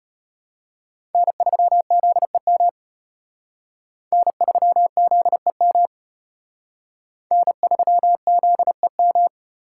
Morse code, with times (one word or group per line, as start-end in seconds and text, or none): 1.1-2.7 N3ZEM
4.1-5.9 N3ZEM
7.3-9.3 N3ZEM